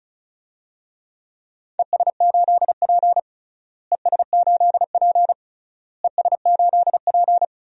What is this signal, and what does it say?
Morse code, heard as a keyed tone, about 36 wpm, 700 Hz. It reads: ES8P ES8P ES8P
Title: DXCC-100 Call Signs - 4 Characters - Repeated 3x - 35wpm